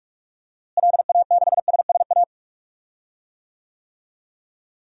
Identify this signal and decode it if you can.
Morse code, heard as a keyed tone, about 42 wpm, 700 Hz. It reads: LA6SSA
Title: DXCC-100 Call Signs - 6 Characters - Repeated 2x - 45wpm